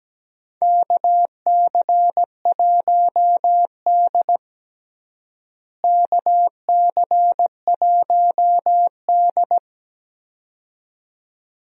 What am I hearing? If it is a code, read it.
Morse code, heard as a keyed tone, about 17 wpm, 700 Hz. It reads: KC1D KC1D
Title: USA Call Signs - 4 Characters - Repeated 2x - 17wpm